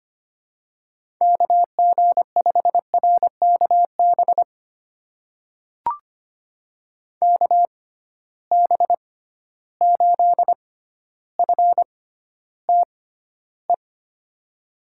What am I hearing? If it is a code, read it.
Morse code, heard as a keyed tone, about 25 wpm, spaced as for 13 wpm, 700 Hz. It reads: KG5RKB KB8FTE
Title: USA Call Signs - 6 Characters - 6x Character Spacing - 25wpm